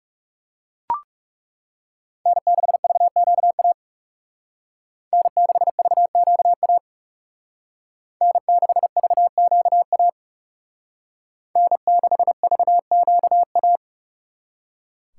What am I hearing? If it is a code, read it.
Morse code, heard as a keyed tone, about 38 wpm, 700 Hz. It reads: N6VQA N6VQA N6VQA N6VQA